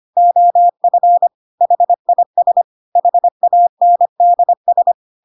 Morse code, heard as a keyed tone, about 25 wpm, 700 Hz. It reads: OF HIS HANDS
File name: Pinocchio / Pinocchio ch2 25 25 ik1hge